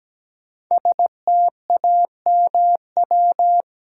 Morse code, heard as a keyed tone, about 17 wpm, 700 Hz. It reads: STAMW
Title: Code Groups - 5 Letters - 17wpm